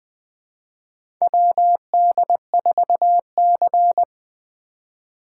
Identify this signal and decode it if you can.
Morse code, heard as a keyed tone, about 20 wpm, 700 Hz. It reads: WD4C
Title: USA Call Signs - 4 Characters - 4x Character Spacing - 20wpm